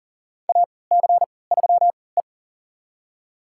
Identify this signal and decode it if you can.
Morse code, heard as a keyed tone, about 40 wpm, spaced as for 23 wpm, 700 Hz. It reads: AC3E
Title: USA Call Signs - 4 Characters - 3x Character Spacing - 40wpm